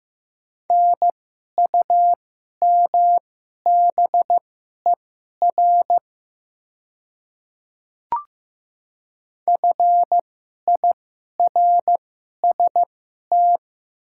Morse code, heard as a keyed tone, about 15 wpm, spaced as for 11 wpm, 700 Hz. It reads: NUMBER FIRST
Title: 2x Character Spacing - Top 300 Words - 15wpm